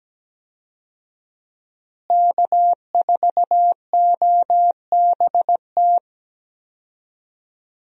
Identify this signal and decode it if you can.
Morse code, heard as a keyed tone, about 17 wpm, 700 Hz. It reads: K4OBT
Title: USA Call Signs - 5 Characters - Repeated 3x - 17wpm